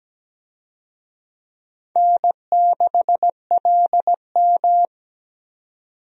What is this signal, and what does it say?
Morse code, heard as a keyed tone, about 17 wpm, 700 Hz. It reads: N6LM
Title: USA Call Signs - 4 Characters - 17wpm